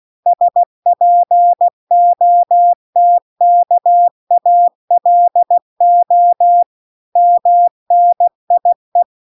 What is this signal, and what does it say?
Morse code, heard as a keyed tone, about 16 wpm, 700 Hz. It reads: SPOTKALO MNIE
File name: 16wpm